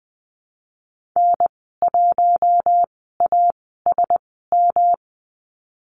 Morse code, heard as a keyed tone, about 20 wpm, spaced as for 15 wpm, 700 Hz. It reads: N1ASM
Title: 186 Lesson 30g - Letter b - Call Signs 20wpm